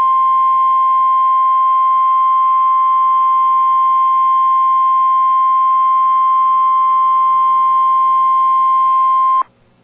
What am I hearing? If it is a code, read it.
Emergency Alert System (SAME) tones (the 1050 Hz attention tone). It is heard from 0.9 to 9.0 s.